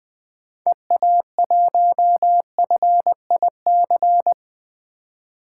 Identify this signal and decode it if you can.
Morse code, heard as a keyed tone, about 20 wpm, 700 Hz. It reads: EA1FIC
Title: 2023 POTA Hunter Call Signs - 20wpm